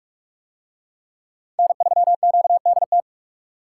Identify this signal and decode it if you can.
Morse code, heard as a keyed tone, about 41 wpm, 700 Hz. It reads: N3QDT